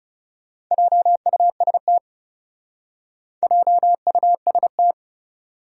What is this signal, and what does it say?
Morse code, heard as a keyed tone, about 36 wpm, 700 Hz. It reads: JUST JUST
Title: Speed-Racing Top 300 Words 30wpm